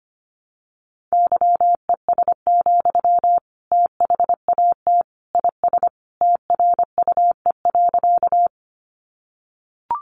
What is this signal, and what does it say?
Morse code, heard as a keyed tone, about 25 wpm, 700 Hz. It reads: YES, THAT IS TRUE.
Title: Sentences from Top 500 Words 25wpm